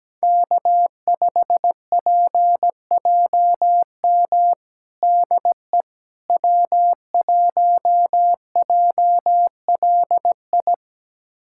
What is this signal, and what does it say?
Morse code, heard as a keyed tone, about 17 wpm, 700 Hz. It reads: K5PJM DE W1JLI
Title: USA Callsign DE Callsign - 5 Characters - 17wpm